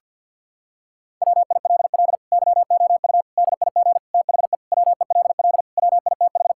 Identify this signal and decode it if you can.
Morse code, heard as a keyed tone, about 50 wpm, 700 Hz. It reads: WILL YOU DIG THE WELL WITH
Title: Sentences from Top 1600 Words 50wpm